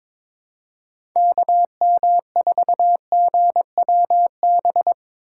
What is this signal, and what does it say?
Morse code, heard as a keyed tone, about 22 wpm, 700 Hz. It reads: KM4GWB
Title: USA Call Signs - 6 Characters - 3x Character Spacing - 22wpm